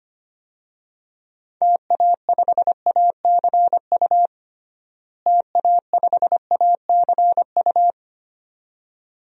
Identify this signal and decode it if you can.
Morse code, heard as a keyed tone, about 25 wpm, 700 Hz. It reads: TA5ACU TA5ACU
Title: DXCC-100 Call Signs - 6 Characters - Repeated 2x - 25wpm